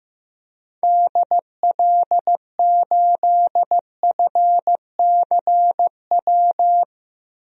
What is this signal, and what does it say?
Morse code, heard as a keyed tone, about 16 wpm, 700 Hz. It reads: DL8FCW
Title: DXCC-100 Call Signs - 6 Characters - 15wpm